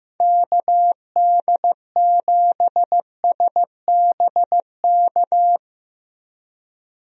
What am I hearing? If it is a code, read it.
Morse code, heard as a keyed tone, about 15 wpm, 700 Hz. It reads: KD7SBK
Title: USA Call Signs - 6 Characters - Repeated 3x - 15wpm